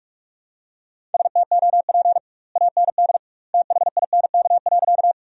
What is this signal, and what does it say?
Morse code, heard as a keyed tone, about 46 wpm, 700 Hz. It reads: STOP AND THINK.
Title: 2x Word-Spacing - Sentences from Top 300 Words - 45wpm